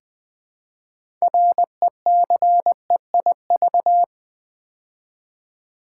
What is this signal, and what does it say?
Morse code, heard as a keyed tone, about 20 wpm, 700 Hz. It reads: RECEIV
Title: Build A Word - 2 Letters At A Time - 8-Letter Words - 20wpm